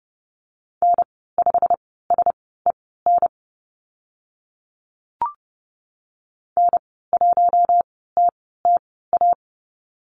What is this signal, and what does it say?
Morse code, heard as a keyed tone, about 30 wpm, spaced as for 17 wpm, 700 Hz. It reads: N5SEN N1TTA
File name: Lesson 15g - Number 5 - Call Signs 30wpm